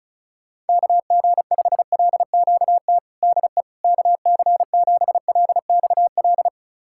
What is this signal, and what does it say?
Morse code, heard as a keyed tone, about 34 wpm, 700 Hz. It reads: KG5LQT DE KC7LXL
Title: USA Callsign DE Callsign - 6 Characters - 35wpm